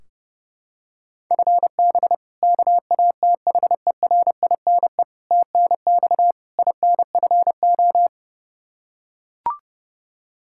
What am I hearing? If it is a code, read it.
Morse code, heard as a keyed tone, about 32 wpm, 700 Hz. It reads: FB KATHERINE TNX INFO
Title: Practice QSO - Exchange 3 - More Info - 30wpm